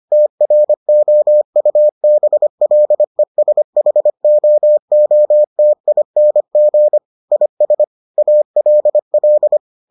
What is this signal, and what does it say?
Morse code, heard as a keyed tone, about 25 wpm, 600 Hz. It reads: TROUBLESHOOTING IS ALL